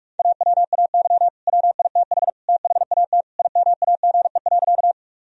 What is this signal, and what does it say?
Morse code, heard as a keyed tone, about 43 wpm, 700 Hz. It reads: AWAY WITH THAT IMAGE.
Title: Sentences from Top 1000 Words - Review Part 4 - 45wpm